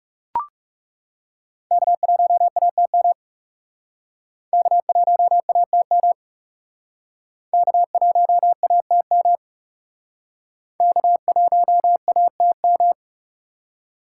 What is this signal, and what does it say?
Morse code, heard as a keyed tone, about 37 wpm, 700 Hz. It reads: K1ATM K1ATM K1ATM K1ATM
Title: Speed Racing - USA 1x3 Call Signs 30wpm